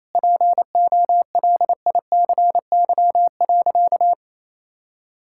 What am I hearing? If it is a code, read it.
Morse code, heard as a keyed tone, about 28 wpm, 700 Hz. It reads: POLICY.
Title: Sentences from Top 1000 Words - Review Part 5 - 28wpm